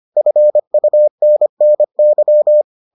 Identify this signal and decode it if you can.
Morse code, heard as a keyed tone, about 25 wpm, 600 Hz. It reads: FUNNY